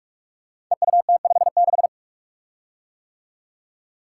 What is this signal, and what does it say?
Morse code, heard as a keyed tone, about 44 wpm, 700 Hz. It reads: EUT56
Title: Code Groups - 5 Letter:Numbers - 45wpm